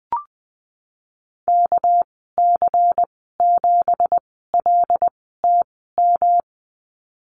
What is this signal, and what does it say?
Morse code, heard as a keyed tone, about 20 wpm, spaced as for 15 wpm, 700 Hz. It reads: KC7LTM
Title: 213 Lesson 34g - Number 8 - Call Signs 20wpm